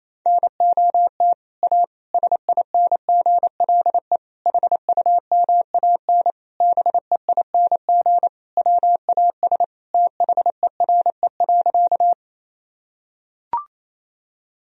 Morse code, heard as a keyed tone, about 28 wpm, 700 Hz. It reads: NOT A SINGLE HUMAN BEING WAS THERE.
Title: Sentences from Top 1000 Words - Review Part 4 - 28wpm